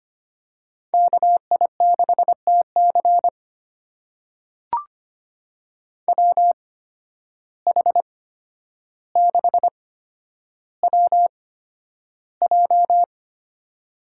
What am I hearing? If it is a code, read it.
Morse code, heard as a keyed tone, about 25 wpm, spaced as for 11 wpm, 700 Hz. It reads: KI6TC WH6WJ
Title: USA Call Signs - 5 Characters - 8x Character Spacing - 25wpm